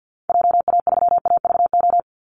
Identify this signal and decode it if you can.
Morse code, heard as a keyed tone, about 50 wpm, 700 Hz. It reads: JA3AVO
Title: Call Signs - 50wpm